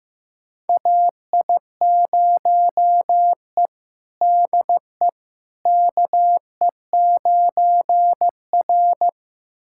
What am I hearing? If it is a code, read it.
Morse code, heard as a keyed tone, about 15 wpm, 700 Hz. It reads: AI0E DE KE9R
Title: USA Callsign DE Callsign - 4 Characters - 15wpm